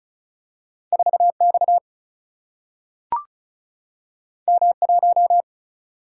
Morse code, heard as a keyed tone, about 36 wpm, 700 Hz. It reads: 4X M1
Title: Single-Letter Single-Number 35wpm